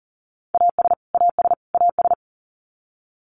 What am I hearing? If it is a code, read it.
Morse code, heard as a keyed tone, about 40 wpm, 700 Hz. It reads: AS AS AS
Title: Top 300 Words Repeated 40wpm